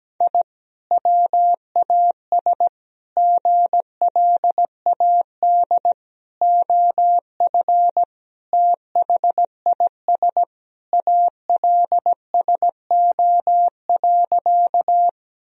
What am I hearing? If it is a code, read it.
Morse code, heard as a keyed tone, about 17 wpm, 700 Hz. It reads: I WAS GLAD OF THIS ALSO.
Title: Sentences from Top 1000 Words - Review Part 1 - 17wpm